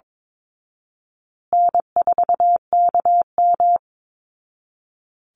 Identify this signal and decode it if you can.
Morse code, heard as a keyed tone, about 22 wpm, 700 Hz. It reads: N4KM
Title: Call Signs Repeated 2x - 22wpm